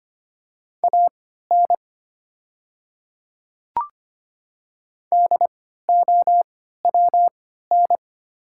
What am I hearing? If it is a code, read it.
Morse code, heard as a keyed tone, about 25 wpm, spaced as for 14 wpm, 700 Hz. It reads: AN DOWN
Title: 3x Character Spacing - Top 100 Words - 25wpm